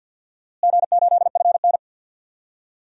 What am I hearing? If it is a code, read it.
Morse code, heard as a keyed tone, about 52 wpm, 700 Hz. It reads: G8FN
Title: DXCC-100 Call Signs - 4 Characters - Repeated 3x - 50wpm